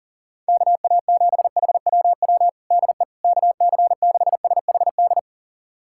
Morse code, heard as a keyed tone, about 40 wpm, 700 Hz. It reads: KA7HWW DE KC6SHD
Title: USA Callsign DE Callsign 40wpm